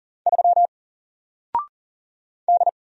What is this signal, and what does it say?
Morse code, heard as a keyed tone, about 38 wpm, 700 Hz. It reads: 3 D
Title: Single Character (Letter:Number:Punctuation:Prosign) 40wpm